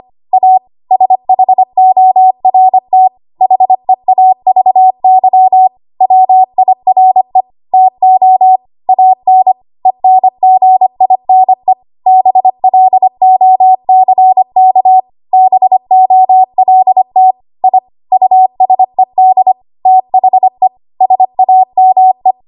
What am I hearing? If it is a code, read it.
Morse code, heard as a keyed tone, about 25 wpm, 750 Hz. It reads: A SHORT HEAVY WIRE TO AN ENGINE BLOCK BOLT I USED THE SAME
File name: Archive / 25 WPM / 25WPM